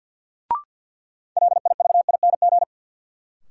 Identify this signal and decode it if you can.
Morse code, heard as a keyed tone, about 53 wpm, 700 Hz. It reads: LIVING
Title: Sentences from Top 1100 Words - New Word Drill Alt - 50wpm